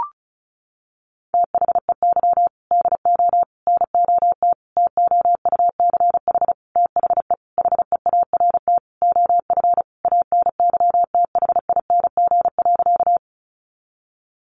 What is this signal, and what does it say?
Morse code, heard as a keyed tone, about 34 wpm, 700 Hz. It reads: THEY DO NOT TOUCH THE HEART OF ANYTHING.
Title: Sentences from Top 1200 Words 35wpm